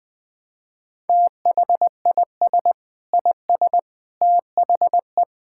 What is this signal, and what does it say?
Morse code, heard as a keyed tone, about 20 wpm, 700 Hz. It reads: THIS IS THE
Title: Sentences with 8-Word Phrases - 20wpm